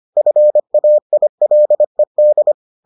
Morse code, heard as a keyed tone, about 25 wpm, 600 Hz. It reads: FAILED